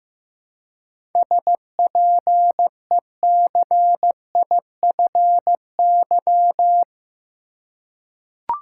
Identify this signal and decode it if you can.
Morse code, heard as a keyed tone, about 16 wpm, 700 Hz. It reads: SPECIFY
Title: Seven Letter Words - 15wpm